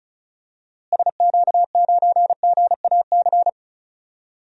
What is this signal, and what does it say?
Morse code, heard as a keyed tone, about 36 wpm, 700 Hz. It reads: SQ9GAC
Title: DXCC-100 Call Signs - 6 Characters - Repeated 3x - 35wpm